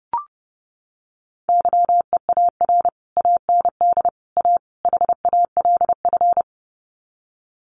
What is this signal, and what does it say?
Morse code, heard as a keyed tone, about 30 wpm, 700 Hz. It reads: YEAR AND A HALF
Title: Sets of 4 Words 30wpm